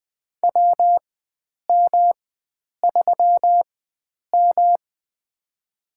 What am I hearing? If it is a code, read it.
Morse code, heard as a keyed tone, about 20 wpm, spaced as for 9 wpm, 700 Hz. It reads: WM3M
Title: USA Call Signs - 4 Characters - 4x Character Spacing - 20wpm